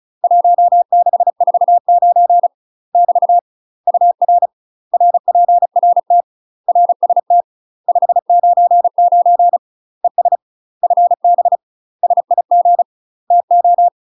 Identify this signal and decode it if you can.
Morse code, heard as a keyed tone, about 34 wpm, 700 Hz. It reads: 1649 <BT> UR RPRT RST 599 ES FB SIG TO
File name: UK MASTER QSOs TRAINING / 35WPM / UK MASTER QSOs 0000